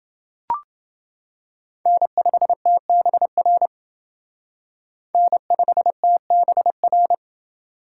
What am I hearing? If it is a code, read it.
Morse code, heard as a keyed tone, about 28 wpm, 700 Hz. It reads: N5TBR N5TBR